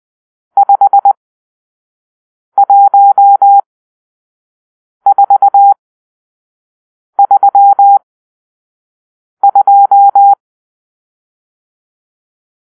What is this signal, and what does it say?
Morse code, heard as a keyed tone, about 20 wpm, spaced as for 5 wpm, 800 Hz. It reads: 51432
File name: arrl / Grp-6 Drill-2